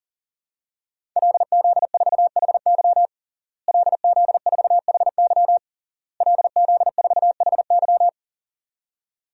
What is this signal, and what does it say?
Morse code, heard as a keyed tone, about 40 wpm, 700 Hz. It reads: LZ4HY LZ4HY LZ4HY
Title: DXCC-100 Call Signs - 5 Characters - Repeated 3x - 40wpm